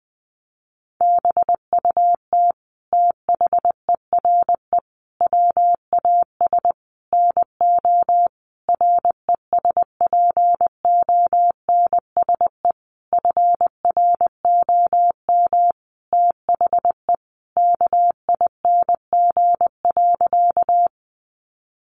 Morse code, heard as a keyed tone, about 20 wpm, 700 Hz. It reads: BUT THERE WAS NO RESPONSE FROM THE KING.
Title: Sentences from Top 1400 Words 20wpm